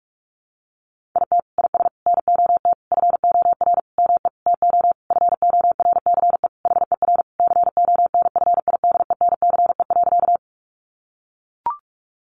Morse code, heard as a keyed tone, about 44 wpm, 700 Hz. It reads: IT IS NOT FOR ME TO FORCE HER CONFIDENCE.
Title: Sentences from Top 1500 Words 45wpm